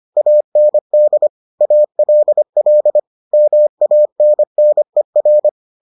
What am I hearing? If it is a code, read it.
Morse code, heard as a keyed tone, about 25 wpm, 600 Hz. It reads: AND ALL MANNER